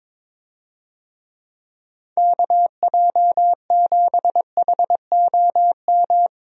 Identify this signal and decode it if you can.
Morse code, heard as a keyed tone, about 21 wpm, 700 Hz. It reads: KJ7HOM
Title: USA Call Signs - 6 Characters - 22wpm